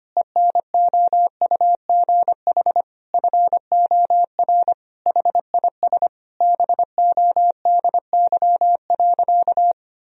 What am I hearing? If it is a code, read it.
Morse code, heard as a keyed tone, about 25 wpm, 700 Hz. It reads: ENOUGH FOR HIS BODY.
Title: Sentences from Top 1000 Words - Review Part 2 - 25wpm